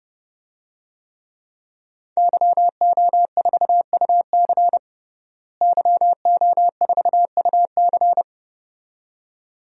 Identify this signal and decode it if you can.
Morse code, heard as a keyed tone, about 30 wpm, 700 Hz. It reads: YO4UC YO4UC
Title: DXCC-100 Call Signs - 5 Characters - Repeated 2x - 30wpm